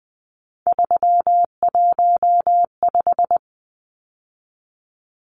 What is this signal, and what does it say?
Morse code, heard as a keyed tone, about 20 wpm, 700 Hz. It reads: 315
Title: Numbers - 3-Digit 20wpm